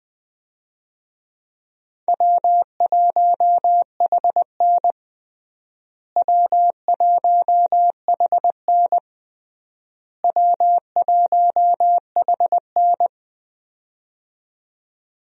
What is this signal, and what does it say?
Morse code, heard as a keyed tone, about 20 wpm, 700 Hz. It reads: W1HN W1HN W1HN
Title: USA Call Signs - 4 Characters - Repeated 3x - 20wpm